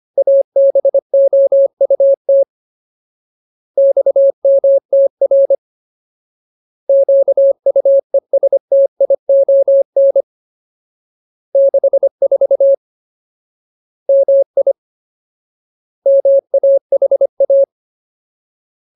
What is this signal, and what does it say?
Morse code, heard as a keyed tone, about 25 wpm, 550 Hz. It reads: ABOUT XMTR QUESTION 64 MI MAHA
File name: Ultimate Word List / 25WPM Ultimate